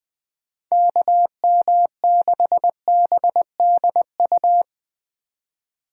Morse code, heard as a keyed tone, about 20 wpm, 700 Hz. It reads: KM6BDU